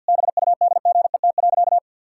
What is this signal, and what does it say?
Morse code, heard as a keyed tone, about 50 wpm, 700 Hz. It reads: BUDGET.
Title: Pangram Challenge v2 - 50wpm